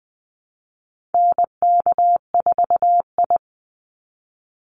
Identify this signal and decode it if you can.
Morse code, heard as a keyed tone, about 20 wpm, 700 Hz. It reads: NK4I
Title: Call Signs Repeated 3x - 20wpm